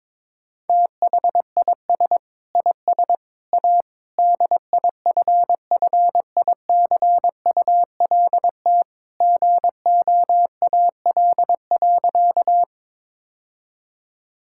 Morse code, heard as a keyed tone, about 22 wpm, 700 Hz. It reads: THIS IS A DIFFICULT GOAL.
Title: Sentences from Top 1000 Words - Review Part 5 - 22wpm